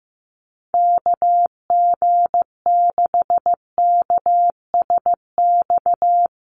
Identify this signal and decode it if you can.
Morse code, heard as a keyed tone, about 15 wpm, 700 Hz. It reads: KG6KSX
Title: USA Call Signs Repeated 3x 15wpm